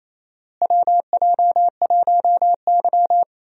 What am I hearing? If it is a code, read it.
Morse code, heard as a keyed tone, about 28 wpm, 700 Hz. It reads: WJ1Y
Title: USA Call Signs - 4 Characters - Repeated 3x - 28wpm